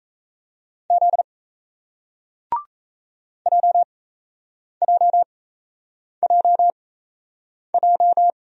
Morse code, heard as a keyed tone, about 37 wpm, spaced as for 8 wpm, 700 Hz. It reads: Z JJJJ